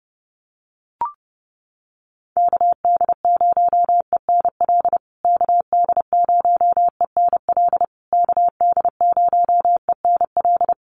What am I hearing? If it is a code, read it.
Morse code, heard as a keyed tone, about 30 wpm, 700 Hz. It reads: KD0ENL KD0ENL KD0ENL
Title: USA Call Signs Repeated 3x 30wpm